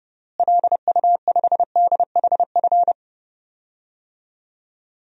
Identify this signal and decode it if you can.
Morse code, heard as a keyed tone, about 30 wpm, 700 Hz. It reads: LU5DHF
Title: DXCC-100 Call Signs - 6 Characters - Repeated 3x - 30wpm